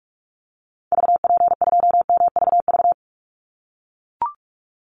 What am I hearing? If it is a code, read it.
Morse code, heard as a keyed tone, about 44 wpm, 700 Hz. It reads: VP2MVV
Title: Call Signs - 45wpm